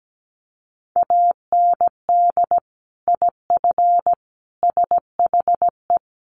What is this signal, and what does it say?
Morse code, heard as a keyed tone, about 17 wpm, 700 Hz. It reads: AND IF SHE
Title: Sentences from Top 1300 Words 17wpm